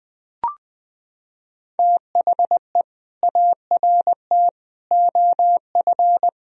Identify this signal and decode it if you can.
Morse code, heard as a keyed tone, about 20 wpm, 700 Hz. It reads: THE ART OF
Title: Sentences from Top 1000 Words - Review Part 3 - 20wpm